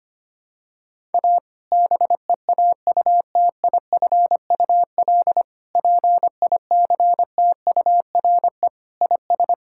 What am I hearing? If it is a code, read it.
Morse code, heard as a keyed tone, about 25 wpm, 700 Hz. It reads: A BEAUTIFUL PICTURE IS
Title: Sentences from Top 1000 Words - Review Part 6 - 25wpm